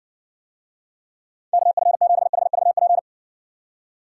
Morse code, heard as a keyed tone, about 68 wpm, 700 Hz. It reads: KF7HVP